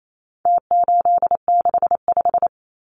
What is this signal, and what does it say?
Morse code, heard as a keyed tone, about 28 wpm, 700 Hz. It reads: T865
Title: Numbers - Contest Style - 4-Digit 28wpm